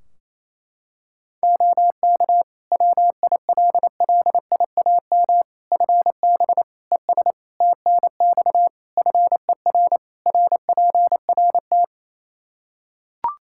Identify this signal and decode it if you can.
Morse code, heard as a keyed tone, about 28 wpm, 700 Hz. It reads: OK WILLIAM FB ES TNX FER RPRT
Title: Practice QSO - Exchange 2 - Info - 28wpm